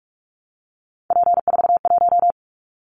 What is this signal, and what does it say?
Morse code, heard as a keyed tone, about 44 wpm, 700 Hz. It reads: P41
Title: Callsign Prefixes for Top 100 DXCC - 45wpm